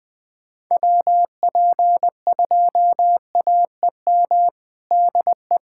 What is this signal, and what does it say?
Morse code, heard as a keyed tone, about 20 wpm, 700 Hz. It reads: WP2AEM DE
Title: USA Callsign DE Callsign - 6 Characters - 20wpm